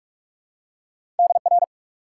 Morse code, heard as a keyed tone, about 44 wpm, 700 Hz. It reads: DR
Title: Top 90 QSO Elements 45wpm